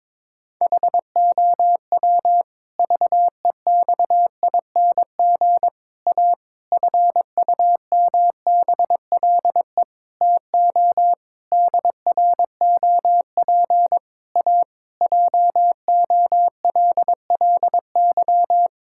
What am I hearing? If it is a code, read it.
Morse code, heard as a keyed tone, about 22 wpm, 700 Hz. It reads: HOW VEXING A FUMBLE TO DROP A JOLLY
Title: Pangram Challenge v2 - 22wpm